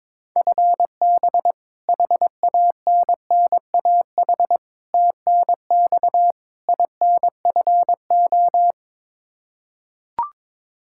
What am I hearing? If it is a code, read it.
Morse code, heard as a keyed tone, about 22 wpm, 700 Hz. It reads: FB HANNAH TNX INFO
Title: Practice QSO - Exchange 3 - More Info - 22wpm